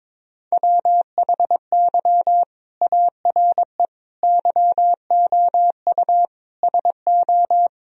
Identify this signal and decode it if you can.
Morse code, heard as a keyed tone, about 22 wpm, 700 Hz. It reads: WHY ARE YOU SO
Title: Sentences from Top 1000 Words - Review Part 2 - 22wpm